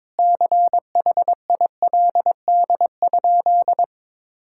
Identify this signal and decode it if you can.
Morse code, heard as a keyed tone, about 22 wpm, 700 Hz. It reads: CHILD?
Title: Sentences from Top 1600 Words 22wpm